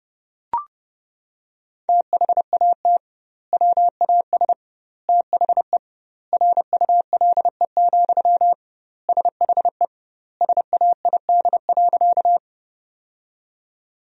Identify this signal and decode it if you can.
Morse code, heard as a keyed tone, about 30 wpm, 700 Hz. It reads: THAT WAS THE RULE, SHE SAID.
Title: 2x Word-Spacing - Sentences from Top 400 Words - 30wpm